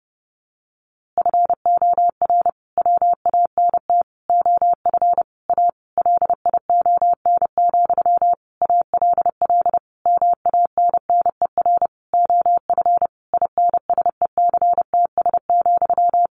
Morse code, heard as a keyed tone, about 30 wpm, 700 Hz. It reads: FOR WANT OF A LION, ALL MANNER OF INSECTS,
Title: A Sound Of Thunder 30wpm